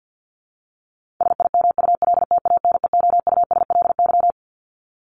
Morse code, heard as a keyed tone, about 50 wpm, 700 Hz. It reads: SIMULTANEOUSLY
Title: LY Suffix - Top 100 Words with Base Words 50wpm